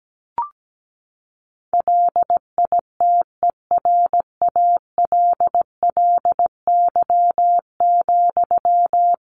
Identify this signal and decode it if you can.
Morse code, heard as a keyed tone, about 17 wpm, 700 Hz. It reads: LITERALLY,
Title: Sentences from Top 1400 Words 17wpm